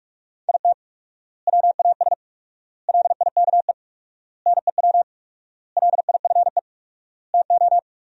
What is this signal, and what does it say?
Morse code, heard as a keyed tone, about 43 wpm, 700 Hz. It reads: IT WAS LIKE NEW LIFE TO
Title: 4x Word-Spacing - Sentences from Top 300 Words - 45wpm